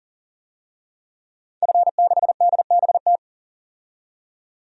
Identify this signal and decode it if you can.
Morse code, heard as a keyed tone, about 40 wpm, 700 Hz. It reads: F6DBT
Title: DXCC-100 Call Signs - 5 Characters - 40wpm